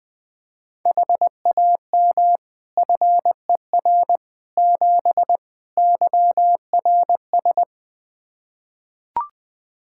Morse code, heard as a keyed tone, about 21 wpm, 700 Hz. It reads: HAM FER 7 YRS
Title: Practice QSO - Exchange 3 - More Info - 20wpm